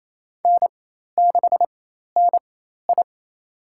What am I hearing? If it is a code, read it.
Morse code, heard as a keyed tone, about 28 wpm, spaced as for 13 wpm, 700 Hz. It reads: N6NI
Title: USA Call Signs - 4x Character Spacing - 28wpm